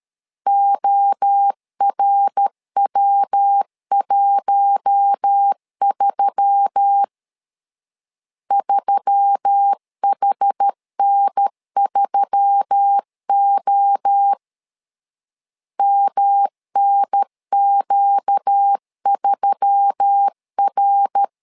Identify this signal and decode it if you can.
Morse code, heard as a keyed tone, about 13 wpm, 800 Hz. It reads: ORW13 3HN3O MNQ3R
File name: LEZIONI / 4 Q1S3 12wpm 2023